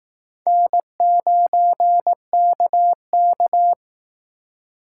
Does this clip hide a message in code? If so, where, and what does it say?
Yes: Morse code from 0.5 to 3.7 s, heard as a keyed tone, about 20 wpm, 700 Hz. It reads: N9KK